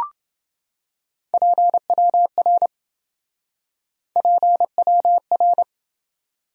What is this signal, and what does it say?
Morse code, heard as a keyed tone, about 27 wpm, 700 Hz. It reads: PWR PWR